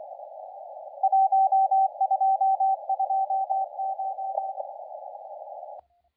Morse code, heard as a keyed tone, about 24 wpm, 770 Hz, with radio noise. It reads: 1229
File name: la4c / sac-cw-2017 / CONTEST AUDIO 1100Z